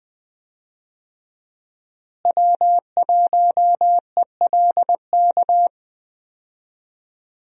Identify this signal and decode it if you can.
Morse code, heard as a keyed tone, about 20 wpm, 700 Hz. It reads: W1ELK
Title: USA Call Signs - 5 Characters - Repeated 3x - 20wpm